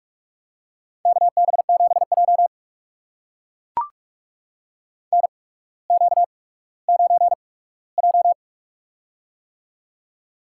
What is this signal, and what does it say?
Morse code, heard as a keyed tone, about 45 wpm, spaced as for 19 wpm, 700 Hz. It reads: KB7J NQ9J
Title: USA Call Signs - 4 Characters - 8x Character Spacing - 45wpm